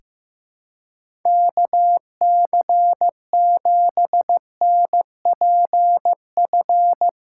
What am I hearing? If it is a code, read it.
Morse code, heard as a keyed tone, about 15 wpm, 700 Hz. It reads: KC7NPF
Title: USA Callsign DE Callsign - 6 Characters - 15wpm